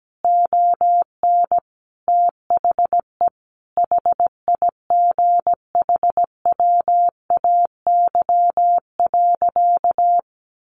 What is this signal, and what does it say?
Morse code, heard as a keyed tone, about 17 wpm, 700 Hz. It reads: ON THE HIGHWAY.
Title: Sentences from Top 1400 Words 17wpm